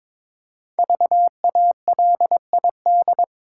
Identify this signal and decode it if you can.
Morse code, heard as a keyed tone, about 22 wpm, 700 Hz. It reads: VALID
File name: 5 Letter Words - First 3 Letters Given - 22wpm